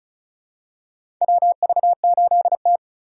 Morse code, heard as a keyed tone, about 36 wpm, 700 Hz. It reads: WV8T
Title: USA Call Signs - 4 Characters - 2x Character Spacing - 35wpm